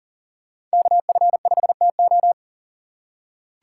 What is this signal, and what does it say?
Morse code, heard as a keyed tone, about 38 wpm, 700 Hz. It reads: KF5TO
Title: USA Call Signs - 8x Character Spacing - 40wpm